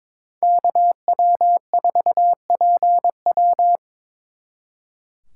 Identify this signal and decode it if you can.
Morse code, heard as a keyed tone, about 26 wpm, 700 Hz. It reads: KW4PW
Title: Speed Racing - USA 2x2 Call Signs 22wpm